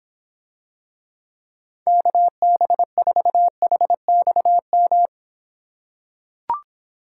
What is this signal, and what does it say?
Morse code, heard as a keyed tone, about 26 wpm, 700 Hz. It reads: KB4HXM